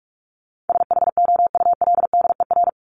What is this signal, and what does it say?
Morse code, heard as a keyed tone, about 44 wpm, 700 Hz. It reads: SHOULDER
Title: Top 1501-1750 Words 45wpm